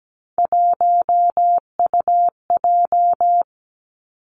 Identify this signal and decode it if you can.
Morse code, heard as a keyed tone, about 17 wpm, 700 Hz. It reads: 1UJ
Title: Three Challenging Letter-Number Combinations 17wpm